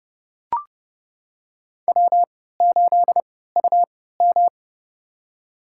Morse code, heard as a keyed tone, about 32 wpm, spaced as for 17 wpm, 700 Hz. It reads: W8UM
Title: USA Call Signs - 4 Characters - 3x Character Spacing - 30wpm